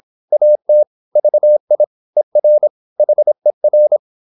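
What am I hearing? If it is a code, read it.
Morse code, heard as a keyed tone, about 26 wpm, 600 Hz. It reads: AT VI ER HER
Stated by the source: High-speed (QRQ) Morse at 26 wpm.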